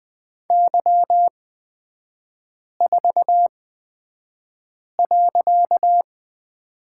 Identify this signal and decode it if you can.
Morse code, heard as a keyed tone, about 20 wpm, spaced as for 5 wpm, 690 Hz. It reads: Y4.
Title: 255 Lesson 41e - Period - Three Characters 20wpm